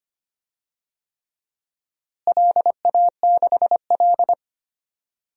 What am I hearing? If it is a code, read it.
Morse code, heard as a keyed tone, about 25 wpm, 700 Hz. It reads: LA6L
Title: DXCC-100 Call Signs - 4 Characters - Repeated 3x - 25wpm